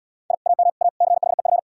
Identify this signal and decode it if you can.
Morse code, heard as a keyed tone, about 71 wpm, 700 Hz. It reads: I AM A CHEF